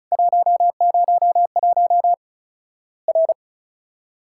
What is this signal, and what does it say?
Morse code, heard as a keyed tone, about 36 wpm, 700 Hz. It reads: 101 R
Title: Worked All Europe DX Contest - QTC Exchange - 35wpm